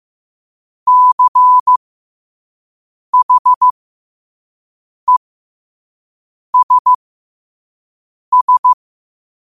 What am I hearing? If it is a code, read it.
Morse code, heard as a keyed tone, about 15 wpm, spaced as for 5 wpm, 1000 Hz. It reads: CHESS